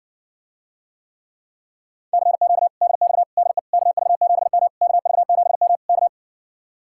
Morse code, heard as a keyed tone, about 59 wpm, 700 Hz. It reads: CQ DX DE KV7M KV7M K